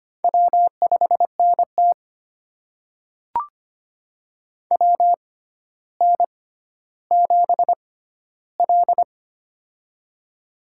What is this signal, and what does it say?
Morse code, heard as a keyed tone, about 25 wpm, spaced as for 13 wpm, 700 Hz. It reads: W5NT WN7L
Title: USA Call Signs - 4 Characters - 6x Character Spacing - 25wpm